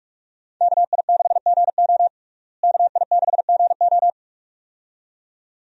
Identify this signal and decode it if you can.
Morse code, heard as a keyed tone, about 43 wpm, 700 Hz. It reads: KI6GO KI6GO
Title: USA Call Signs - 5 Characters - Repeated 2x - 45wpm